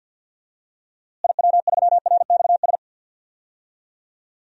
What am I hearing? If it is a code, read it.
Morse code, heard as a keyed tone, about 51 wpm, 700 Hz. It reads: IW3RXS
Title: DXCC-100 Call Signs - 6 Characters - Repeated 3x - 50wpm